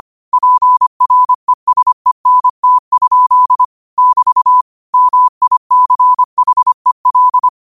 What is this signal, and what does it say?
Morse code, heard as a keyed tone, about 25 wpm, 1000 Hz. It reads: PRESENT? <BT> MICHEL
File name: qotdfiles / 25wpm